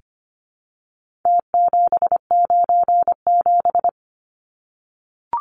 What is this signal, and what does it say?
Morse code, heard as a keyed tone, about 25 wpm, 700 Hz. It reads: T797
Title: Numbers - Contest Style - 4-Digit 25wpm